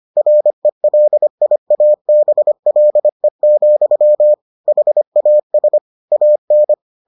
Morse code, heard as a keyed tone, about 25 wpm, 600 Hz. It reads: RELIABLE, HAS AN